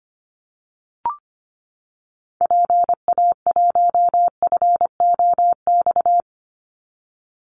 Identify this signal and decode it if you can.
Morse code, heard as a keyed tone, about 25 wpm, 700 Hz. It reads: PA1FOX
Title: Call Signs - 25wpm